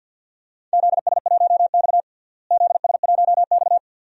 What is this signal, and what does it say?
Morse code, heard as a keyed tone, about 52 wpm, 700 Hz. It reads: ZS1X ZS1X
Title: DXCC-100 Call Signs - 4 Characters - Repeated 2x - 50wpm